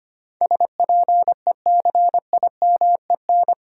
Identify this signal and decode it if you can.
Morse code, heard as a keyed tone, about 25 wpm, 700 Hz. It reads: SPECIMEN
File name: Eight Letter Words - 25wpm